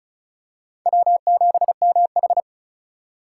Morse code, heard as a keyed tone, about 36 wpm, 700 Hz. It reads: W7MH
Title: USA Call Signs - 4 Characters - 3x Character Spacing - 35wpm